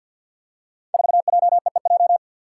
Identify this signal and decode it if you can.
Morse code, heard as a keyed tone, about 53 wpm, 700 Hz. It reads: 4JEEJ
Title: Five Challenging Letter-Number Combinations 50wpm